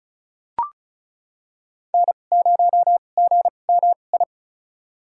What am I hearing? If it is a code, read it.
Morse code, heard as a keyed tone, about 36 wpm, spaced as for 25 wpm, 700 Hz. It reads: N0GMI
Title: USA Call Signs - 5 Characters - 2x Character Spacing - 35wpm